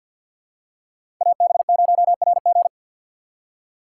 Morse code, heard as a keyed tone, about 47 wpm, 700 Hz. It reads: AB0RG
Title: USA Call Signs - 5 Characters - 4x Character Spacing - 50wpm